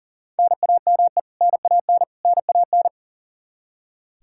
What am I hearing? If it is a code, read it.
Morse code, heard as a keyed tone, about 40 wpm, 700 Hz. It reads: NAME NAN NAN
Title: Practice QSO - Exchange 1 - Beginning - 40wpm